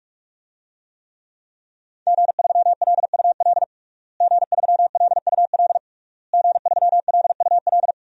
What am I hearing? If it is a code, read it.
Morse code, heard as a keyed tone, about 46 wpm, 700 Hz. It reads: G3LUL G3LUL G3LUL
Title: DXCC-100 Call Signs - 5 Characters - Repeated 3x - 45wpm